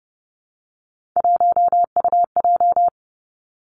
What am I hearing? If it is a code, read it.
Morse code, heard as a keyed tone, about 30 wpm, 700 Hz. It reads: 1UJ
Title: Three Challenging Letter-Number Combinations 30wpm